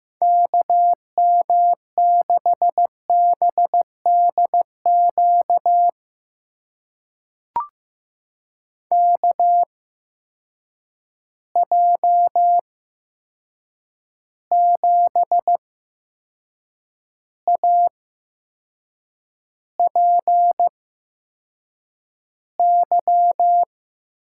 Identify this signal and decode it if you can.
Morse code, heard as a keyed tone, about 15 wpm, spaced as for 6 wpm, 700 Hz. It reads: KM6BDQ KJ7APY
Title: USA Call Signs - 6 Characters - 8x Character Spacing - 15wpm